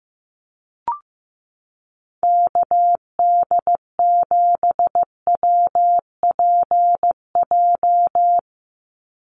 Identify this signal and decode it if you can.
Morse code, heard as a keyed tone, about 15 wpm, 700 Hz. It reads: KD7WPJ
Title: NA SOTA Activator Callsigns 15wpm